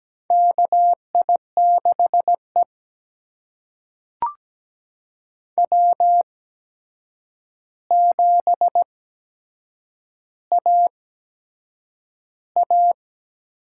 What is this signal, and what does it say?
Morse code, heard as a keyed tone, about 17 wpm, spaced as for 7 wpm, 700 Hz. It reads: KI6E W7AA
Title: USA Call Signs - 8x Character Spacing - 17wpm